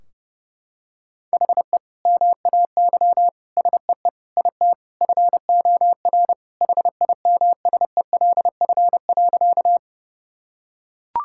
Morse code, heard as a keyed tone, about 32 wpm, 700 Hz. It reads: HE MAY SEE IT FOR HIMSELF.
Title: Sentences from Top 1600 Words 30wpm